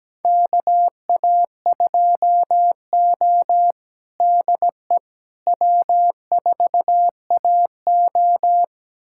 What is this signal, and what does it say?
Morse code, heard as a keyed tone, about 17 wpm, 700 Hz. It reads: KA2O DE W4AO
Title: USA Callsign DE Callsign - 4 Characters - 17wpm